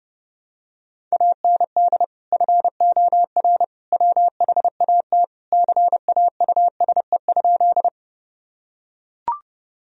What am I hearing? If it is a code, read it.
Morse code, heard as a keyed tone, about 30 wpm, 700 Hz. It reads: AND FOR WHAT CAUSE?
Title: Sentences from Top 1000 Words - Review Part 1 - 30wpm